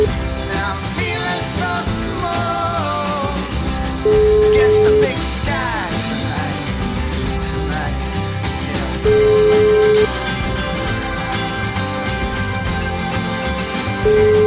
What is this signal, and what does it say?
Ringback tone (Europe, 425 Hz), with noise.